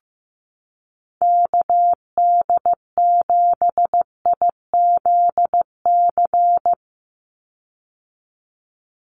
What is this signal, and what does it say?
Morse code, heard as a keyed tone, about 15 wpm, 700 Hz. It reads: KD7IZC
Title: USA Call Signs 15wpm